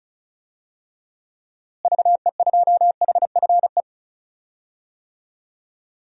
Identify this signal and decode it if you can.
Morse code, heard as a keyed tone, about 36 wpm, 700 Hz. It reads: VE2HFE
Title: DXCC-100 Call Signs - 6 Characters - Repeated 2x - 35wpm